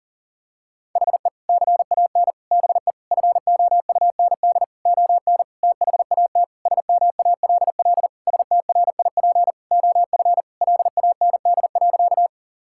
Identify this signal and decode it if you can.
Morse code, heard as a keyed tone, about 40 wpm, 700 Hz. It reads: HE CAN BE FOUND ON THAT SMALL STRIP OF LAND.
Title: Sentences from Top 1600 Words 40wpm